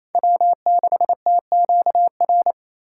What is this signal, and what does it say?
Morse code, heard as a keyed tone, about 35 wpm, 700 Hz. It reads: W6TQR